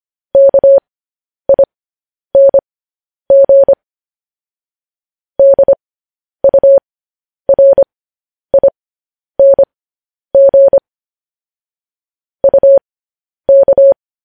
Morse code, heard as a keyed tone, about 25 wpm, spaced as for 10 wpm, 550 Hz. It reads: KING DURING UK
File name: podcast / News headlines-1x-25x10